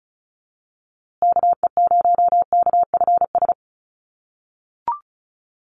Morse code, heard as a keyed tone, about 34 wpm, 700 Hz. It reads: KE0KFS
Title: USA Call Signs Repeated 2x 35wpm